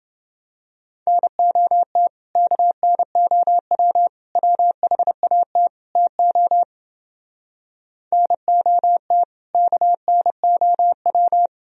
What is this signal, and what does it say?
Morse code, heard as a keyed tone, about 29 wpm, 700 Hz. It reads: NOT KNOW WHAT TO NOT KNOW